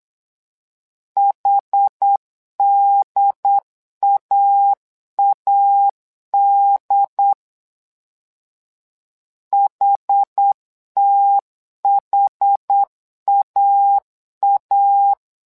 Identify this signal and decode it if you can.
Morse code, heard as a keyed tone, about 8 wpm, 800 Hz. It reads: HDAAD HTHAA
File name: LEZIONI / 2 UHDN / 2 UHD 8wpm 2258